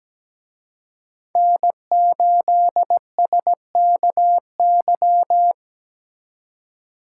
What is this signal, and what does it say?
Morse code, heard as a keyed tone, about 20 wpm, 700 Hz. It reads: N8SKY
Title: Speed Racing - USA Call Signs 17wpm